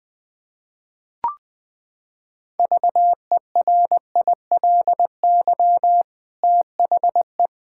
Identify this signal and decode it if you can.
Morse code, heard as a keyed tone, about 20 wpm, 700 Hz. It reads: VERILY THE
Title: Pangram Challenge v2 - 20wpm